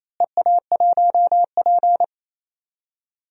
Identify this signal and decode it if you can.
Morse code, heard as a keyed tone, about 28 wpm, 700 Hz. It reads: EA1P
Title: DXCC-100 Call Signs - 4 Characters - Repeated 3x - 28wpm